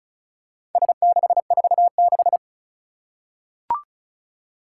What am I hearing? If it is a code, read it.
Morse code, heard as a keyed tone, about 36 wpm, 690 Hz. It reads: S646
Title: Four Challenging Letter-Number Combinations 35wpm